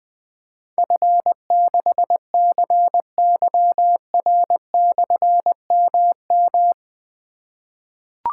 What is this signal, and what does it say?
Morse code, heard as a keyed tone, about 20 wpm, 700 Hz. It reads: F6CYR/MM
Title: Difficult Call Signs 20wpm